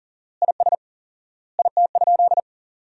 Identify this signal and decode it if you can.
Morse code, heard as a keyed tone, about 40 wpm, 700 Hz. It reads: IS IT?
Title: 4x Word-Spacing - Sentences from Top 200 Words - 40wpm